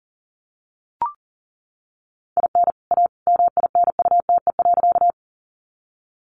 Morse code, heard as a keyed tone, about 40 wpm, 700 Hz. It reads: IN A MINUTE.
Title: Sentences from Top 500 Words 40wpm